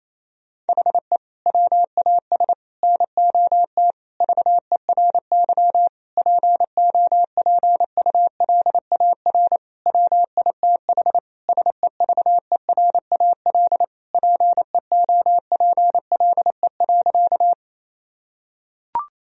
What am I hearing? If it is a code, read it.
Morse code, heard as a keyed tone, about 28 wpm, 700 Hz. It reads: HE WAS NOT VERY POPULAR WITH SEVERAL PEOPLE.
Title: Sentences from Top 1000 Words - Review Part 3 - 28wpm